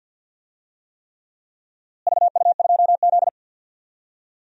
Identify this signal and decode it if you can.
Morse code, heard as a keyed tone, about 50 wpm, 700 Hz. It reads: VU2Z
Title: DXCC-100 Call Signs - 4 Characters - Repeated 3x - 50wpm